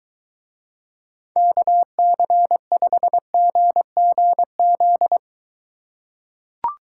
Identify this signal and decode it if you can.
Morse code, heard as a keyed tone, about 23 wpm, 700 Hz. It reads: KC5GGZ